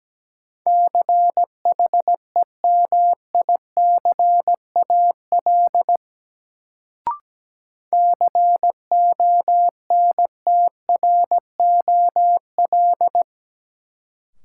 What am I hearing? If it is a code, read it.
Morse code, heard as a keyed tone, about 17 wpm, 700 Hz. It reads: CHEMICAL CONTROL
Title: Sentences from Top 1200 Words - New Word Drill Alt - 17wpm